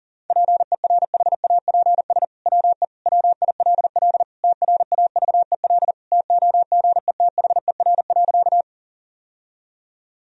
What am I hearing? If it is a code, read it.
Morse code, heard as a keyed tone, about 40 wpm, 700 Hz. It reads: PERHAPS WE WILL TRAVEL TOGETHER.
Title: Sentences from Top 1000 Words - Review Part 2 - 40wpm